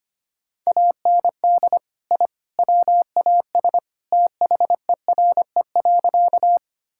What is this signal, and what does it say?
Morse code, heard as a keyed tone, about 25 wpm, 700 Hz. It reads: AND I WAS THERE.
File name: practice / 25wpm / Sentences with 3-Word Phrases - 25wpm